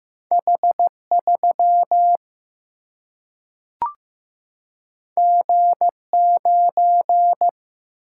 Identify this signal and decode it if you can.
Morse code, heard as a keyed tone, about 15 wpm, 700 Hz. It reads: H3 G9
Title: Single-Letter Single-Number 15wpm